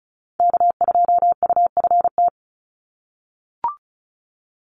Morse code, heard as a keyed tone, about 34 wpm, 700 Hz. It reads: K2UFT
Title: Call Signs - 35wpm